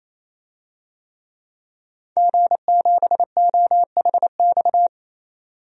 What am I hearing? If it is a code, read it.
Morse code, heard as a keyed tone, about 27 wpm, 700 Hz. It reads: G7OHX
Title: DXCC-100 Call Signs - 5 Characters - 28wpm